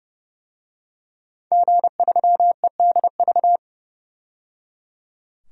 Morse code, heard as a keyed tone, about 30 wpm, 700 Hz. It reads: G3EDV
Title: DXCC-100 Call Signs - 5 Characters - 30wpm